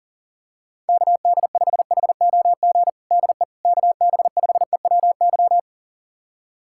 Morse code, heard as a keyed tone, about 40 wpm, 700 Hz. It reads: KD5HOG DE KB5EWY
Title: USA Callsign DE Callsign - 6 Characters - 40wpm